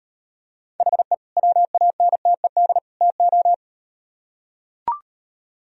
Morse code, heard as a keyed tone, about 38 wpm, 700 Hz. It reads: HE WANTED TO